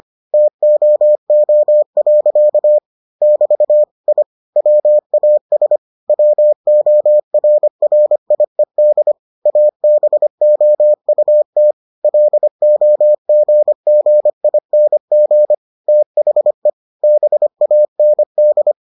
Morse code, heard as a keyed tone, about 25 wpm, 600 Hz. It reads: TOO. <BT> I WAS WORRIED ABOUT LOGGING THE BAND